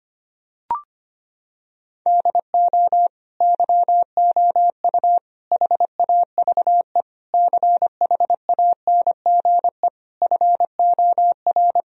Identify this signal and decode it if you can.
Morse code, heard as a keyed tone, about 25 wpm, 700 Hz. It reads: DO YOU HAVE CHANGE FOR
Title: Sentences from Top 1000 Words - Review Part 2 - 25wpm